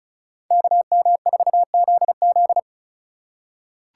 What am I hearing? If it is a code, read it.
Morse code, heard as a keyed tone, about 36 wpm, 700 Hz. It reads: KM4ZZ
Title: K1USN SST QSO Practice - 35wpm